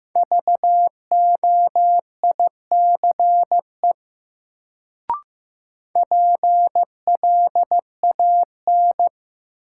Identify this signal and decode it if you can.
Morse code, heard as a keyed tone, about 15 wpm, 700 Hz. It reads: VOICE PLAN
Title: Sentences from Top 500 Words - New Word Drill - 15wpm